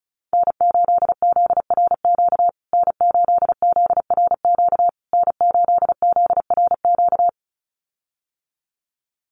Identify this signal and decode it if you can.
Morse code, heard as a keyed tone, about 34 wpm, 700 Hz. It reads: N8ZRQ N8ZRQ N8ZRQ
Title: USA Call Signs Repeated 3x 35wpm